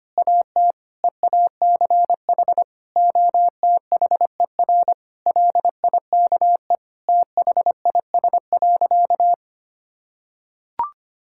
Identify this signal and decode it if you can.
Morse code, heard as a keyed tone, about 25 wpm, 700 Hz. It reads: AT EACH OTHER LIKE THIS.
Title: Sentences from Top 1000 Words - Review Part 6 - 25wpm